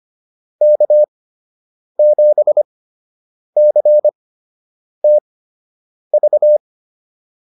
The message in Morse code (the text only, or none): K7CTV